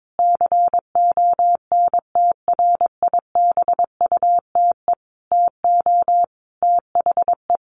Morse code, heard as a keyed tone, about 22 wpm, 700 Hz. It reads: CONTRIBUTE TO THE
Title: Sentences from Top 1400 Words 22wpm